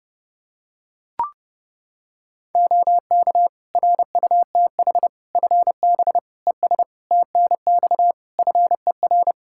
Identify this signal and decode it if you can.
Morse code, heard as a keyed tone, about 30 wpm, 700 Hz. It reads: OK RUTH FB ES TNX FER
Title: Practice QSO - Exchange 2 - Info - 30wpm